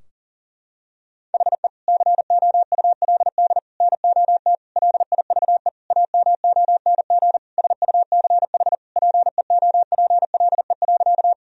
Morse code, heard as a keyed tone, about 40 wpm, 700 Hz. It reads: HE COULD NOT LIVE AMONG SUCH PEOPLE.
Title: Sentences from Top 1000 Words - Review Part 3 - 40wpm